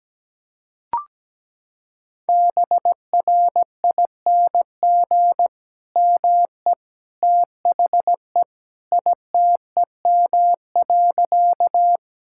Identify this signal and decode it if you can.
Morse code, heard as a keyed tone, about 17 wpm, 700 Hz. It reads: BRING ME THE ITEM.
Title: Sentences from Top 1000 Words - Review Part 6 - 17wpm